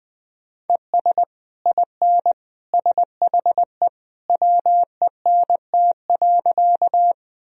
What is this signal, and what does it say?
Morse code, heard as a keyed tone, about 20 wpm, 700 Hz. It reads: ES IN SHE WENT.
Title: Sentences from Top 200 Words and Abbreviations 20wpm